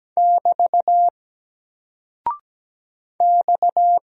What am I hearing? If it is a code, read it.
Morse code, heard as a keyed tone, about 17 wpm, 700 Hz. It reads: <BT> X
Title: Single Character (Letter:Number:Punctuation:Prosign) Alt 17wpm